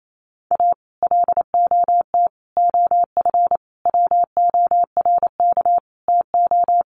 Morse code, heard as a keyed tone, about 28 wpm, 700 Hz. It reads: A LOT OF WORK TO
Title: Sets of 5 Words Encore2 28wpm